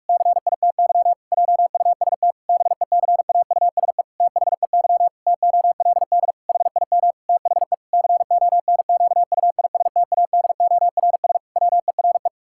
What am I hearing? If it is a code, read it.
Morse code, heard as a keyed tone, about 47 wpm, 700 Hz. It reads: CITY JUST BECAUSE THEY TOLD HIM THE CONQUISTADORS WERE
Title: Pangram Challenge v2 - 45wpm